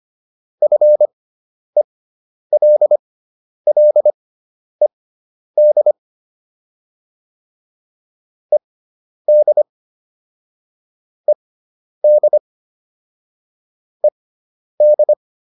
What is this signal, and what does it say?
Morse code, heard as a keyed tone, about 25 wpm, spaced as for 10 wpm, 600 Hz. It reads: FELLED ED ED ED